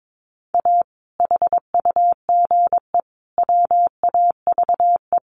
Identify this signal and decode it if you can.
Morse code, heard as a keyed tone, about 22 wpm, 700 Hz. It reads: A HUGE WAVE
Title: Sentences from Top 1400 Words 22wpm